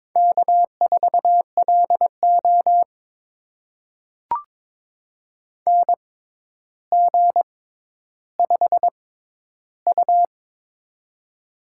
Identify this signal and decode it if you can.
Morse code, heard as a keyed tone, about 23 wpm, spaced as for 11 wpm, 700 Hz. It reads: K4LO NG5U
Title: USA Call Signs - 4 Characters - 6x Character Spacing - 22wpm